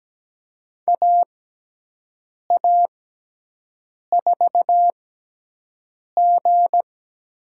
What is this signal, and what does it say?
Morse code, heard as a keyed tone, about 17 wpm, spaced as for 6 wpm, 700 Hz. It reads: AA4G
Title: USA Call Signs - 5 Characters - 6x Character Spacing - 17wpm